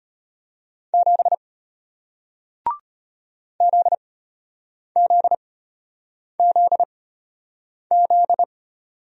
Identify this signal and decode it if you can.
Morse code, heard as a keyed tone, about 33 wpm, spaced as for 8 wpm, 700 Hz. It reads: 7 ZZZZ